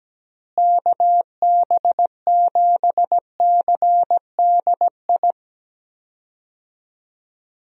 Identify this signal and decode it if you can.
Morse code, heard as a keyed tone, about 17 wpm, 700 Hz. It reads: KB7CDI
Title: USA Call Signs - 6 Characters - Repeated 3x - 17wpm